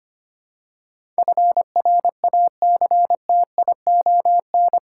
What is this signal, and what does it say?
Morse code, heard as a keyed tone, about 25 wpm, 700 Hz. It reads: FRACTION
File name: Eight Letter Words - 25wpm